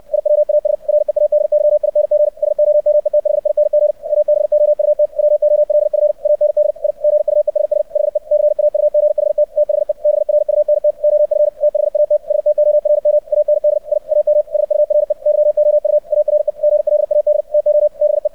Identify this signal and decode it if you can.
Morse code, heard as a keyed tone, about 64 wpm, 600 Hz. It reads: IWAS READYING SOMETHING ABOUT WOLF AND I GUESS HE CAUGHT THE BRUNT OF THAT STORM AND I AM SURE YOU ARE GLAD TO BE
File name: archived / CRSnet